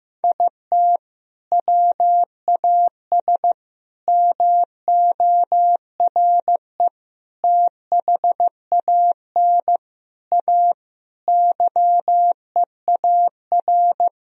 Morse code, heard as a keyed tone, about 15 wpm, 700 Hz. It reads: IT WAS MORE THAN A YEAR
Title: Sentences from Top 1000 Words - Review Part 3 - 15wpm